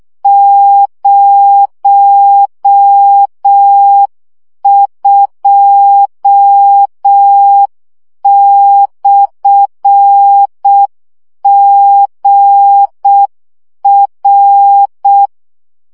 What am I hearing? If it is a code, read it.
Morse code, heard as a keyed tone, about 6 wpm, 800 Hz. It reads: 02/GR
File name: morze vizsga 2011-2 5